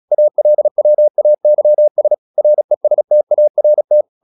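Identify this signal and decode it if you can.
Morse code, heard as a keyed tone, about 36 wpm, 600 Hz. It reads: ALWAYS RESTART
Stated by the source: Morse practice at 36 wpm, text of Ken Follett's novel Never, from CW med Gnister.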